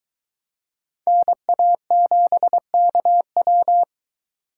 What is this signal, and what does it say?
Morse code, heard as a keyed tone, about 23 wpm, 700 Hz. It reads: NA7KW